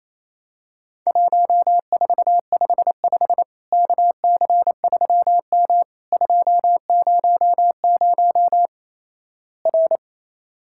Morse code, heard as a keyed tone, about 28 wpm, 700 Hz. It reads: 1455 KC3M 200 R
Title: Worked All Europe DX Contest - QTC Exchange - 28wpm